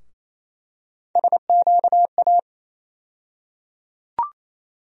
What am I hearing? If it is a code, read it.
Morse code, heard as a keyed tone, about 28 wpm, 700 Hz. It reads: SQA
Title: USA Call Sign Postfixes - 3 Letters 28wpm